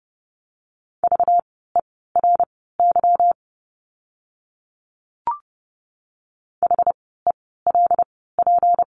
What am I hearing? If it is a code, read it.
Morse code, heard as a keyed tone, about 30 wpm, spaced as for 17 wpm, 700 Hz. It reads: VERY HELP
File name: course / 30:10wpm / 237 Lesson 38f - Letter q - Words 30wpm